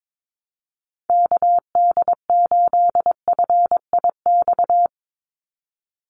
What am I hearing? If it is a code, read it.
Morse code, heard as a keyed tone, about 22 wpm, 700 Hz. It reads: KD8FIX
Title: USA Call Signs 22wpm